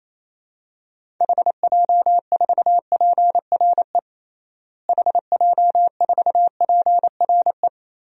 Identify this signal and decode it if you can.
Morse code, heard as a keyed tone, about 28 wpm, 700 Hz. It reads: HJ4PRE HJ4PRE
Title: DXCC-100 Call Signs - 6 Characters - Repeated 2x - 28wpm